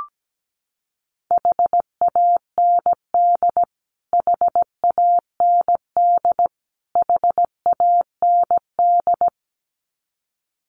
Morse code, heard as a keyed tone, about 17 wpm, 700 Hz. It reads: HAND HAND HAND
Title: Top 200 Words Repeated 17wpm